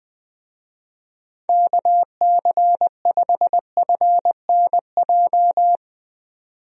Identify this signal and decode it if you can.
Morse code, heard as a keyed tone, about 20 wpm, 700 Hz. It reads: KC5FNJ
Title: USA Call Signs - 6 Characters - 20wpm